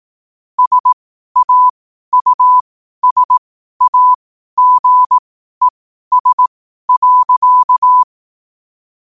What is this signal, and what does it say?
Morse code, heard as a keyed tone, about 18 wpm, spaced as for 12 wpm, 1000 Hz. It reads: SAUSAGES.
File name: qotdfiles / 12wpm